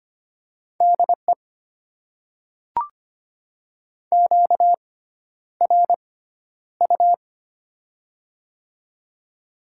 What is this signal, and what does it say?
Morse code, heard as a keyed tone, about 25 wpm, spaced as for 11 wpm, 690 Hz. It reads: DE QRU
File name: Top 90 QSO Elements - 6x Character Spacing - 25wpm